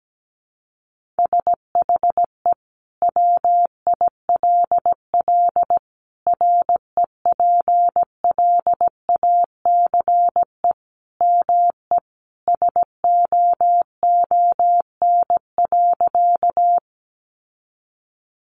Morse code, heard as a keyed tone, about 17 wpm, 700 Hz. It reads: SHE WILL REPLACE ME SOON.
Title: Sentences from Top 1400 Words 17wpm